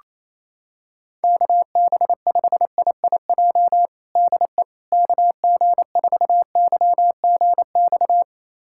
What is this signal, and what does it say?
Morse code, heard as a keyed tone, about 27 wpm, 700 Hz. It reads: KB5IIJ DE KG4YGX
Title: USA Callsign DE Callsign - 6 Characters - 28wpm